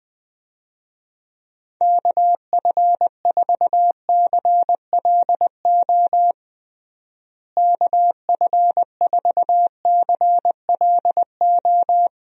Morse code, heard as a keyed tone, about 20 wpm, 700 Hz. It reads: KF4CLO KF4CLO
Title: USA Call Signs - 6 Characters - Repeated 3x - 20wpm